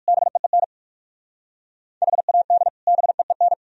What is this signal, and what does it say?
Morse code, heard as a keyed tone, about 60 wpm, 700 Hz. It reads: BEEN HAD BEEN